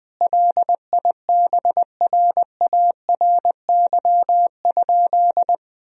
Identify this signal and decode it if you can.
Morse code, heard as a keyed tone, about 20 wpm, 700 Hz. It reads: LIBRARY?
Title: Sentences from Top 1000 Words - Review Part 4 - 20wpm